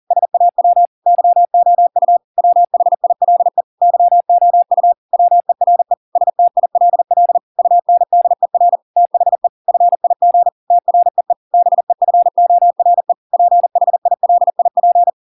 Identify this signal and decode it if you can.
Morse code, heard as a keyed tone, about 40 wpm, 700 Hz. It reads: SAW YOU WHILE YOU WERE STILL UNDER THE FIG TREE BEFORE PHILIP